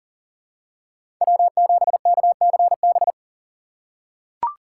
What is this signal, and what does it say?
Morse code, heard as a keyed tone, about 40 wpm, 700 Hz. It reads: W7KCB
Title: USA Call Signs - 5 Characters - 3x Character Spacing - 40wpm